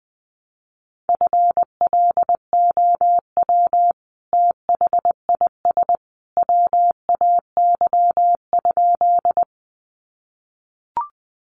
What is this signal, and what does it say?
Morse code, heard as a keyed tone, about 20 wpm, 700 Hz. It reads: FLOW THIS WAY?
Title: Sentences from Top 1200 Words 20wpm